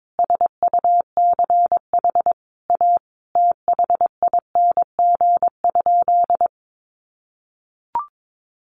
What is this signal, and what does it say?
Morse code, heard as a keyed tone, about 22 wpm, 700 Hz. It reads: SUCH A THING?
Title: Sentences from Top 200 Words 22wpm